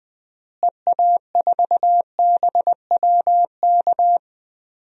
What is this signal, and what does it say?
Morse code, heard as a keyed tone, about 20 wpm, 700 Hz. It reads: EA4BWK
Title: DXCC-100 Call Signs - 6 Characters - Repeated 2x - 20wpm